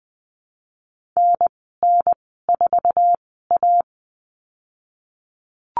Morse code, heard as a keyed tone, about 20 wpm, spaced as for 15 wpm, 700 Hz. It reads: NN4A
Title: 048 Lesson 9g - Number 4 - Call Signs 20wpm